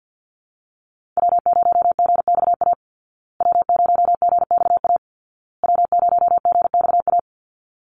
Morse code, heard as a keyed tone, about 50 wpm, 700 Hz. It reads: W0GXA W0GXA W0GXA
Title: Call Signs Repeated 2x - 50wpm